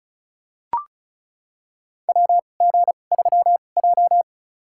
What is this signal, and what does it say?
Morse code, heard as a keyed tone, about 36 wpm, spaced as for 25 wpm, 700 Hz. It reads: WG3J
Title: USA Call Signs - 4 Characters - 2x Character Spacing - 35wpm